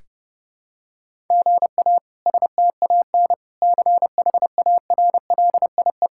Morse code, heard as a keyed tone, about 32 wpm, 700 Hz. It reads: GA STAN CHARLIE
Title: K1USN SST QSO Practice - 30wpm